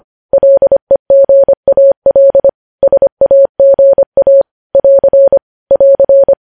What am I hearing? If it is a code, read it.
Morse code, heard as a keyed tone, about 25 wpm, 550 Hz. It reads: LEGAL SAGA <AR> <AR>
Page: news headlines in morse code at 25 wpm